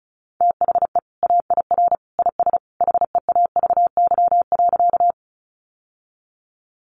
Morse code, heard as a keyed tone, about 34 wpm, 700 Hz. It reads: THE AIR IS HEAVY.
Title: Sentences from Top 1100 Words 35wpm